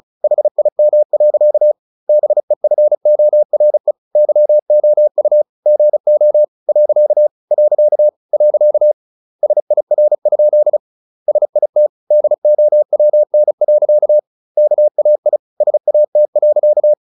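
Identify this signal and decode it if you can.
Morse code, heard as a keyed tone, about 34 wpm, 600 Hz. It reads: HIM. BEFORE YOU GO . . SIR? SIT DOWN. KAI SAT.
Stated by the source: Morse practice at 35 wpm from CW med Gnister, sending Ken Follett's novel Never in English.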